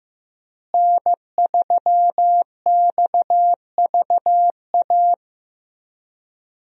Morse code, heard as a keyed tone, about 15 wpm, 700 Hz. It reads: N3XVA
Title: USA Call Signs - 5 Characters - Repeated 3x - 15wpm